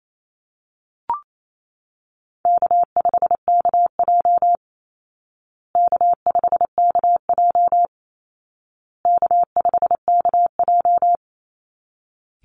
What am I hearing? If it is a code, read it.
Morse code, heard as a keyed tone, about 28 wpm, 700 Hz. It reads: K5KJ K5KJ K5KJ
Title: Call Signs Repeated 2x - 28wpm